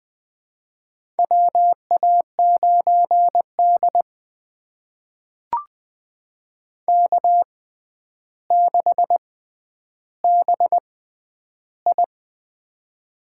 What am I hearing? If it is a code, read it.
Morse code, heard as a keyed tone, about 20 wpm, spaced as for 10 wpm, 700 Hz. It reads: WA9D K6BI
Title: USA Call Signs - 4 Characters - 6x Character Spacing - 20wpm